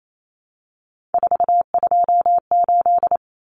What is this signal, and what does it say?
Morse code, heard as a keyed tone, about 28 wpm, 700 Hz. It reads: 428
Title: Common Cut Numbers - 3-Digit 28wpm